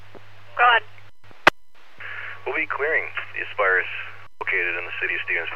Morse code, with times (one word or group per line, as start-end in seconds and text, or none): none